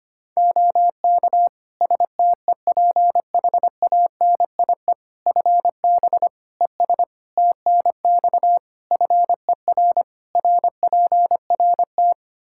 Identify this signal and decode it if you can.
Morse code, heard as a keyed tone, about 25 wpm, 700 Hz. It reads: OK STEPHANIE FB ES TNX FER RPRT
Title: Practice QSO - Exchange 2 - Info - 25wpm